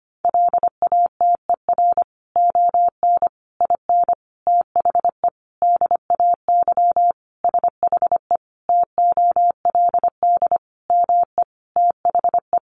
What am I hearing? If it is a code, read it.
Morse code, heard as a keyed tone, about 25 wpm, 700 Hz. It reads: LATER ON IN THE DAY SHE TOLD ME THE
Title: Sentences from Top 700 Words 25wpm